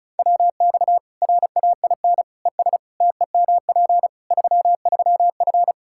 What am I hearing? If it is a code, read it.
Morse code, heard as a keyed tone, about 36 wpm, 700 Hz. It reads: WX RAIN ES TEMP 33F
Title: Practice QSO - Exchange 2 - Info - 35wpm